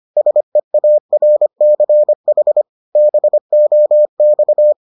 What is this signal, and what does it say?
Morse code, heard as a keyed tone, about 25 wpm, 600 Hz. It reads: SEARCH BOX